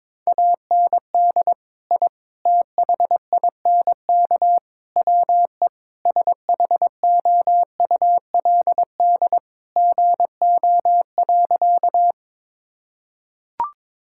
Morse code, heard as a keyed tone, about 22 wpm, 700 Hz. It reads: AND I THINK WE SHOULD GO.
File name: practice / 22wpm / Sentences with 4-Word Phrases - 22wpm